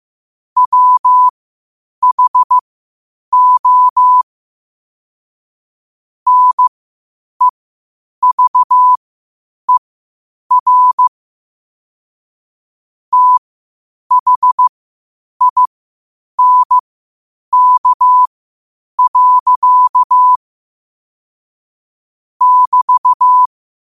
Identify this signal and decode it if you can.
Morse code, heard as a keyed tone, about 15 wpm, spaced as for 8 wpm, 1000 Hz. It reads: WHO NEVER THINK. <BT>
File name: qotdfiles / 8wpm